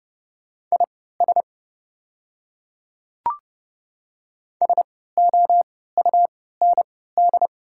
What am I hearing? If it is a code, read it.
Morse code, heard as a keyed tone, about 30 wpm, spaced as for 17 wpm, 700 Hz. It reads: IS SOUND
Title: 3x Character Spacing - Top 100 Words - 30wpm